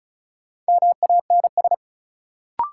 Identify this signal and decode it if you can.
Morse code, heard as a keyed tone, about 36 wpm, 700 Hz. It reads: MANS
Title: Sentences from Top 1000 Words - Review Part 6 - New Word Drill - 35wpm